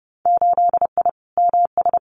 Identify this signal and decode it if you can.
Morse code, heard as a keyed tone, about 30 wpm, 700 Hz. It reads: 8I MS
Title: Winter Field Day Exchanges Encore 30wpm